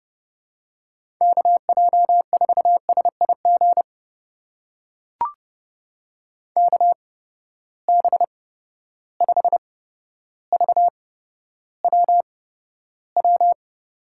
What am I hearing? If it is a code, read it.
Morse code, heard as a keyed tone, about 32 wpm, spaced as for 13 wpm, 700 Hz. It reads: KJ4SIG KB5VWW
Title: USA Call Signs - 6 Characters - 8x Character Spacing - 30wpm